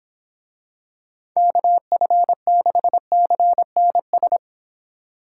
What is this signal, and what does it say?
Morse code, heard as a keyed tone, about 23 wpm, 700 Hz. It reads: KF6CNS